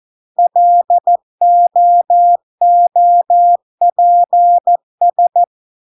Morse code, heard as a keyed tone, about 14 wpm, 700 Hz. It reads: LOOPS